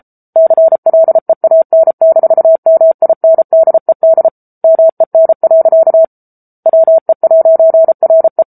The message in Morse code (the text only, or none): CLEAN-MINDED MEN. WE'RE